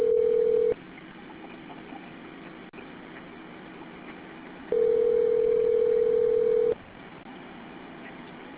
Ringback tone (North America), 440 + 480 Hz.